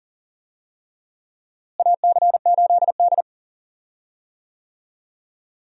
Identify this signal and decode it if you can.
Morse code, heard as a keyed tone, about 40 wpm, 700 Hz. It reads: AC8D
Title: USA Call Signs - 4 Characters - Repeated 3x - 40wpm